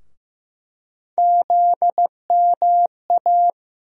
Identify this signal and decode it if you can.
Morse code, heard as a keyed tone, about 15 wpm, 700 Hz. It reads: ZMA
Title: USA Call Sign Postfixes - 3 Letters 15wpm